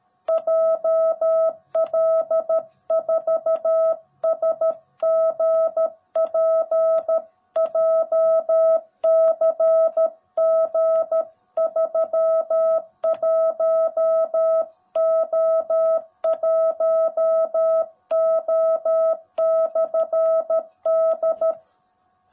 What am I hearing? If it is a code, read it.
Morse code, heard as a keyed tone, about 13 wpm, 650 Hz. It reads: JL4SGP JCG31O1O/D
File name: jh8cbh / 2021cw / cw18b